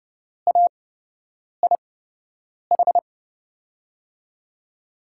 Morse code, heard as a keyed tone, about 30 wpm, spaced as for 8 wpm, 700 Hz. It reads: AIH
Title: USA Call Signs - 6 Characters - 8x Character Spacing - 30wpm